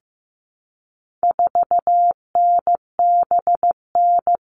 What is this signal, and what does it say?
Morse code, heard as a keyed tone, about 15 wpm, 700 Hz. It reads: 4NBN
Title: All Cut Numbers - 4-Digit 15wpm